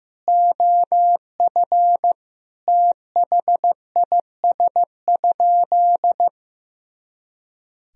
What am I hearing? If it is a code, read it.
Morse code, heard as a keyed tone, about 15 wpm, 700 Hz. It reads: OF THIS?
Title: Sentences from Top 200 Words and Abbreviations 15wpm